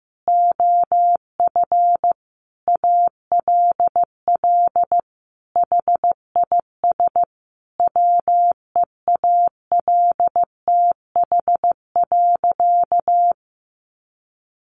Morse code, heard as a keyed tone, about 15 wpm, 700 Hz. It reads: OF ALL HIS WEALTH.
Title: Sentences from Top 1200 Words 15wpm